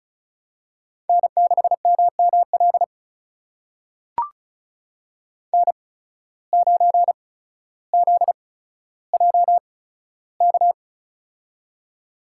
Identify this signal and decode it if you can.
Morse code, heard as a keyed tone, about 36 wpm, spaced as for 15 wpm, 700 Hz. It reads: N6MML N9ZJK
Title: USA Call Signs - 5 Characters - 8x Character Spacing - 35wpm